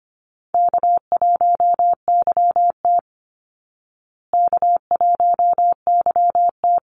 Morse code, heard as a keyed tone, about 25 wpm, 700 Hz. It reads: K1YT K1YT
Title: Easy Call Signs Repeated 2x - 25wpm